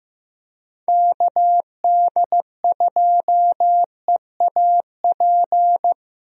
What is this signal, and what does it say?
Morse code, heard as a keyed tone, about 16 wpm, 700 Hz. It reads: KD2EAP
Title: USA Call Signs - 6 Characters - Repeated 2x - 15wpm